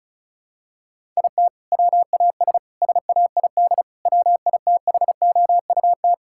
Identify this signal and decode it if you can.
Morse code, heard as a keyed tone, about 36 wpm, 700 Hz. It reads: IT WAS SAID WITHOUT
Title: Sentences from Top 1000 Words - Review Part 6 - 35wpm